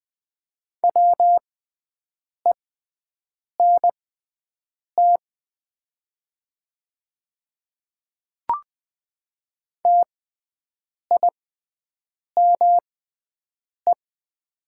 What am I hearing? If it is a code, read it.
Morse code, heard as a keyed tone, about 20 wpm, spaced as for 7 wpm, 700 Hz. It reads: WENT TIME
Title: 6x Character Spacing - Top 300 Words - 20wpm